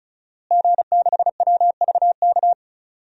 Morse code, heard as a keyed tone, about 36 wpm, 700 Hz. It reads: G6WVK
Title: DXCC-100 Call Signs - 5 Characters - Repeated 3x - 35wpm